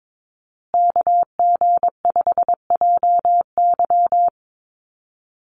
Morse code, heard as a keyed tone, about 22 wpm, 700 Hz. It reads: KG5JY
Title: USA Call Signs Repeated 3x 22wpm